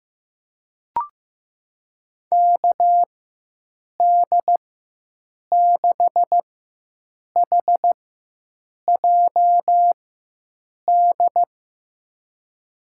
Morse code, heard as a keyed tone, about 16 wpm, spaced as for 7 wpm, 700 Hz. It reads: KD6HJD
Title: USA Call Signs - 6 Characters - 4x Character Spacing - 15wpm